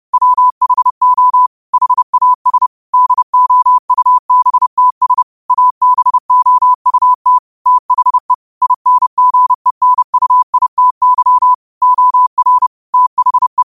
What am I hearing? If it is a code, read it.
Morse code, heard as a keyed tone, about 30 wpm, 1000 Hz. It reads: WHO HAS DOUBTS ABOUT THE INGENUITY OR THE